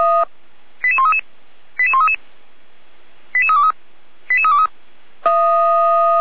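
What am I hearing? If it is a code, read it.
Five-tone selective call (ZVEI), heard as quick runs of five tones, tones of about 70 ms. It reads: 88133 88133 88323 88323
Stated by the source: ZVEI five-tone selective call received by SDR.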